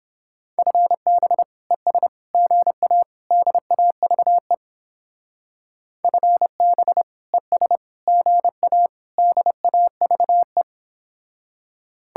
Morse code, heard as a keyed tone, about 31 wpm, 700 Hz. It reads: FB ES GA DAVE FB ES GA DAVE